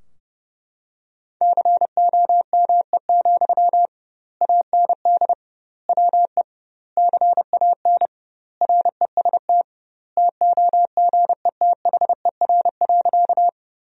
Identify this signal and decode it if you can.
Morse code, heard as a keyed tone, about 32 wpm, 700 Hz. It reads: COME, AND WE CAN REST TOGETHER.
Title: 2x Word-Spacing - Sentences from Top 500 Words - 30wpm